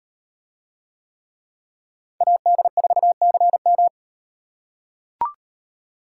Morse code, heard as a keyed tone, about 38 wpm, 700 Hz. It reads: AD4CM